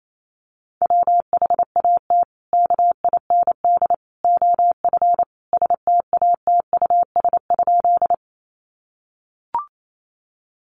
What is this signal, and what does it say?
Morse code, heard as a keyed tone, about 28 wpm, 700 Hz. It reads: WHAT KIND OF STATUS?
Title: Sentences from Top 1400 Words 28wpm